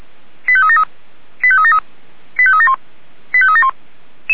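Five-tone selective call (ZVEI), heard as quick runs of five tones, tones of about 70 ms. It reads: 85373 85373 85371 85371
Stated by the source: ZVEI five-tone selective call received by SDR.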